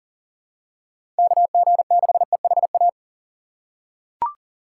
Morse code, heard as a keyed tone, about 40 wpm, 700 Hz. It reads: KG6EHA
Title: USA Call Signs - 6 Characters - 3x Character Spacing - 40wpm